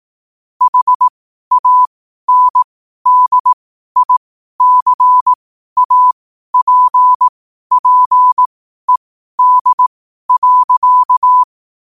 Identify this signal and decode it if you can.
Morse code, heard as a keyed tone, about 18 wpm, spaced as for 12 wpm, 1000 Hz. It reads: HANDICAPPED.
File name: qotdfiles / 12wpm